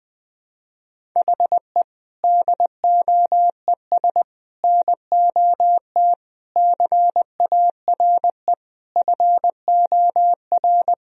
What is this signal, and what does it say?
Morse code, heard as a keyed tone, about 20 wpm, 700 Hz. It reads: HE DOES NOT CARE FOR
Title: Sentences from Top 1000 Words - Review Part 1 - 20wpm